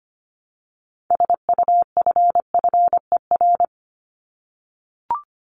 Morse code, heard as a keyed tone, about 25 wpm, 700 Hz. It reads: SUFFER
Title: Top 1501-1750 Words 25wpm